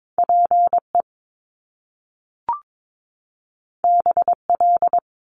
Morse code, heard as a keyed tone, about 22 wpm, 700 Hz. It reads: PE BL
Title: Top Two Letter Combinations in Words 22wpm